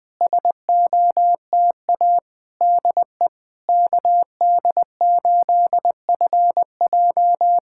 Morse code, heard as a keyed tone, about 20 wpm, 690 Hz. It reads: SOTA DE KD8FJ
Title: SOTA QSO Practice 20wpm